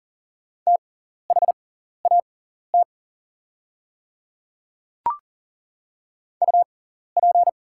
Morse code, heard as a keyed tone, about 40 wpm, spaced as for 14 wpm, 700 Hz. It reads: THAT UP
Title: 6x Character Spacing - Top 100 Words - 40wpm